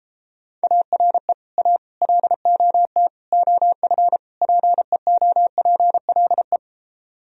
Morse code, heard as a keyed tone, about 33 wpm, 700 Hz. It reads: ARE A LOT OF PEOPLE